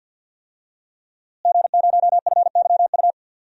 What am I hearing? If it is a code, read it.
Morse code, heard as a keyed tone, about 47 wpm, 700 Hz. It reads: G0FYU
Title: DXCC-100 Call Signs - 5 Characters - Repeated 2x - 50wpm